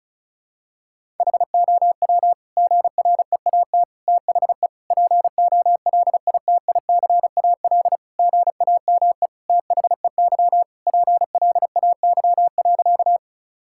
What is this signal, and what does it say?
Morse code, heard as a keyed tone, about 34 wpm, 700 Hz. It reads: HOW GREAT THE POLITICAL GAME THEY PLAY.
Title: Sentences from Top 1000 Words - Review Part 5 - 35wpm